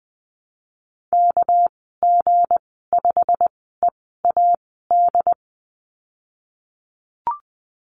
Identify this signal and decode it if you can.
Morse code, heard as a keyed tone, about 20 wpm, spaced as for 15 wpm, 700 Hz. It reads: KG5EAD
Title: 219 Lesson 35g - Number 0 - Call Signs 20wpm